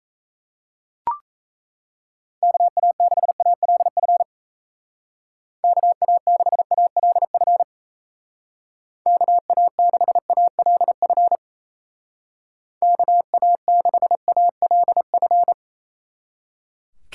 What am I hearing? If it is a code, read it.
Morse code, heard as a keyed tone, about 36 wpm, 700 Hz. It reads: KA6ALF KA6ALF KA6ALF KA6ALF